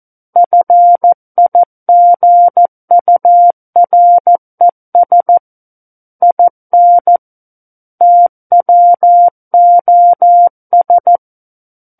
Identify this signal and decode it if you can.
Morse code, heard as a keyed tone, about 14 wpm, 700 Hz. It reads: FIGURES IN TWOS
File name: Book One / 05-Chapter 5 - 14 WPM